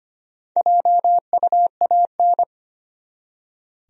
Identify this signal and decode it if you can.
Morse code, heard as a keyed tone, about 25 wpm, 700 Hz. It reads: JUAN
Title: USA Common Names - Max 4 Letters - 25wpm